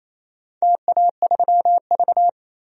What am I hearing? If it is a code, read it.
Morse code, heard as a keyed tone, about 28 wpm, 700 Hz. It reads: TA3V
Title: DXCC-100 Call Signs - 4 Characters - 28wpm